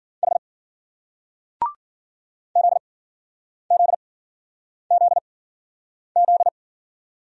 Morse code, heard as a keyed tone, about 56 wpm, 700 Hz. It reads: H Z Z Z Z